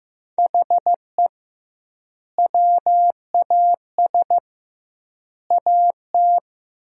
Morse code, heard as a keyed tone, about 15 wpm, 700 Hz. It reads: HE WAS AT
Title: 2x Word-Spacing - Sentences from Top 200 Words - 15wpm